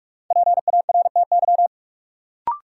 Morse code, heard as a keyed tone, about 46 wpm, 700 Hz. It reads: PARTY
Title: Sentences from Top 1400 Words - New Word Drill - 45wpm